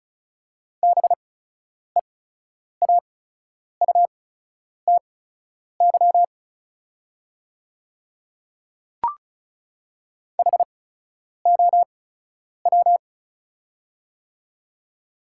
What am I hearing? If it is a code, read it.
Morse code, heard as a keyed tone, about 36 wpm, spaced as for 10 wpm, 690 Hz. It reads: BEAUTY HOW
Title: 8x Character Spacing - Top 500 Words - 35wpm